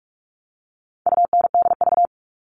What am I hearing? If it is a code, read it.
Morse code, heard as a keyed tone, about 44 wpm, 700 Hz. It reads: UNDV
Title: All Cut Numbers - 4-Digit 45wpm